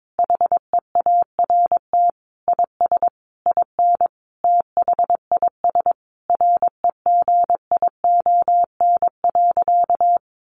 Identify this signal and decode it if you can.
Morse code, heard as a keyed tone, about 22 wpm, 700 Hz. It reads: HEART IS IN THIS REGION.
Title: Sentences from Top 1500 Words 22wpm